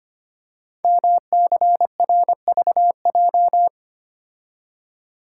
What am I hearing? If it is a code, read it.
Morse code, heard as a keyed tone, about 25 wpm, 700 Hz. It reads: MCRVJ
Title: Code Groups - 5 Letters - 25wpm